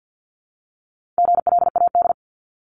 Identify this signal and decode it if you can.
Morse code, heard as a keyed tone, about 50 wpm, 700 Hz. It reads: GLAD
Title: Multi-letter Phonemes 50wpm